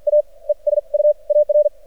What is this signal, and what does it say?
Morse code, heard as a keyed tone, about 53 wpm, 600 Hz. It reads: A E S U AR
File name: archived / CRSnet